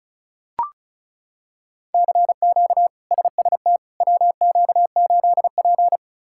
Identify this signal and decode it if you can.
Morse code, heard as a keyed tone, about 34 wpm, 700 Hz. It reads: CQ SST WQ8P
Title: K1USN SST QSO Practice - 35wpm